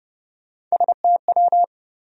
Morse code, heard as a keyed tone, about 30 wpm, 700 Hz. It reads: STW
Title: USA Call Sign Postfixes - 3 Letters 30wpm